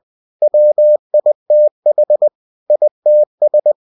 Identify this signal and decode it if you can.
Morse code, heard as a keyed tone, about 20 wpm, 600 Hz. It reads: WITH ITS
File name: cw-books / winnie-the-pooh / winnie-20wpm-page-0114